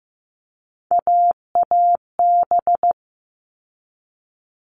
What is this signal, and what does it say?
Morse code, heard as a keyed tone, about 15 wpm, 700 Hz. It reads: AAB
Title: All Cut Numbers - 3-Digit 15wpm